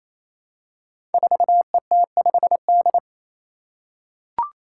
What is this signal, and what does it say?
Morse code, heard as a keyed tone, about 27 wpm, 700 Hz. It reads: 4ET5D
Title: Five Challenging Letter-Number Combinations 28wpm